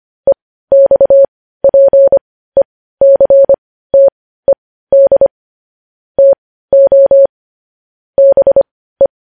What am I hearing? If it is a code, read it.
Morse code, heard as a keyed tone, about 25 wpm, spaced as for 15 wpm, 550 Hz. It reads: EXPECTED TO BE
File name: podcast / News headlines-1x-25x15